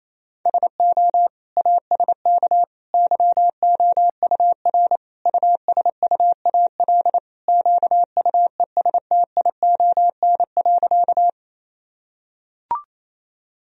Morse code, heard as a keyed tone, about 28 wpm, 700 Hz. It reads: SO ASK YOUR USUAL QUESTION.
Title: Sentences from Top 1000 Words - Review Part 2 - 28wpm